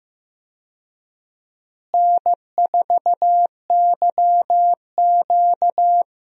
Morse code, heard as a keyed tone, about 15 wpm, 700 Hz. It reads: N4YQ
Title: USA Call Signs - 4 Characters - Repeated 2x - 15wpm